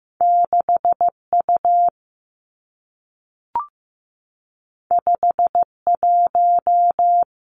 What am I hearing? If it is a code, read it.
Morse code, heard as a keyed tone, about 15 wpm, 700 Hz. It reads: 6U 51
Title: Two Challenging Letter-Number Combinations 15wpm